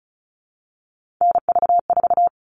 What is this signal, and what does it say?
Morse code, heard as a keyed tone, about 34 wpm, 700 Hz. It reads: NV4
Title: All Cut Numbers - 3-Digit 35wpm